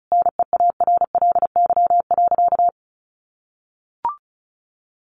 Morse code, heard as a keyed tone, about 34 wpm, 700 Hz. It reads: NEARLY.
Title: Sentences from Top 1300 Words 35wpm